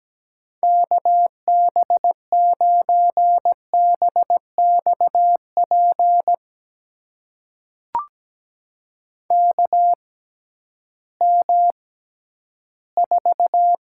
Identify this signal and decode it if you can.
Morse code, heard as a keyed tone, about 17 wpm, spaced as for 11 wpm, 700 Hz. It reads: KB9BXP KM4
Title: USA Call Signs - 6 Characters - 6x Character Spacing - 17wpm